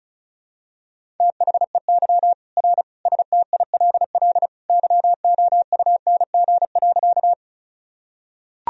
Morse code, heard as a keyed tone, about 36 wpm, 700 Hz. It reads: THEY R STILL YOUNG.
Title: Sentences from Top 400 Words and Abbreviations 35wpm